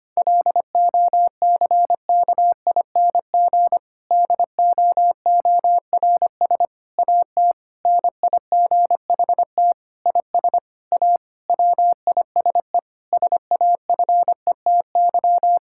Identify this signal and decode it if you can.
Morse code, heard as a keyed tone, about 25 wpm, 700 Hz. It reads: LOCKING DOORS AT NIGHT IS A WISE SAFETY
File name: practice / 25wpm / Difficult Sentences with Hints - 25wpm